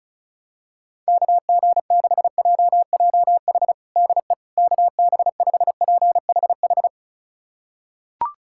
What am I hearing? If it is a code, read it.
Morse code, heard as a keyed tone, about 34 wpm, 700 Hz. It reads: KG6JJH DE KB5PHH
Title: USA Callsign DE Callsign 35wpm